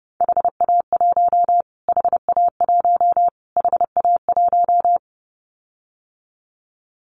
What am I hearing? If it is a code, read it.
Morse code, heard as a keyed tone, about 30 wpm, 700 Hz. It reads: HA1 HA1 HA1
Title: Callsign Prefixes for Top 25 DXCC - Repeated 3x - 30wpm